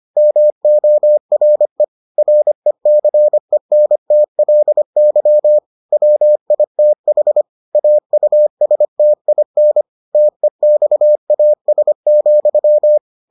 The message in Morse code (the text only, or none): MORE RECENTLY WITH AUSTIN TEXAS,